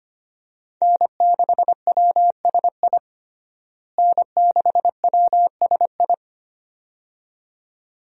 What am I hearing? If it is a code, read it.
Morse code, heard as a keyed tone, about 25 wpm, 700 Hz. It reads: N6WSI N6WSI
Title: USA Call Signs - 5 Characters - Repeated 2x - 25wpm